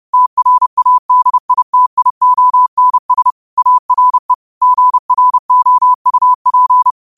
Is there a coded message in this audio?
Yes: Morse code from 0.1 to 6.9 s, heard as a keyed tone, about 30 wpm, 1000 Hz. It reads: TRADITIONS ARE GROUP